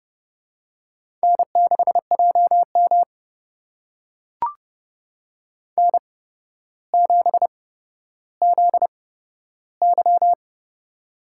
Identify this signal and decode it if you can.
Morse code, heard as a keyed tone, about 30 wpm, spaced as for 13 wpm, 700 Hz. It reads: N6JM N7ZY
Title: USA Call Signs - 4 Characters - 8x Character Spacing - 30wpm